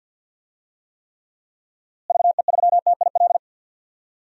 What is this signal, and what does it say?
Morse code, heard as a keyed tone, about 51 wpm, 700 Hz. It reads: VE3TIL
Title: DXCC-100 Call Signs - 6 Characters - Repeated 3x - 50wpm